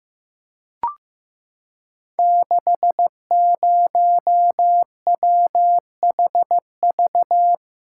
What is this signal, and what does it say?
Morse code, heard as a keyed tone, about 15 wpm, 700 Hz. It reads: 60WHV
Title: Code Groups - 5 Letter:Numbers - 15wpm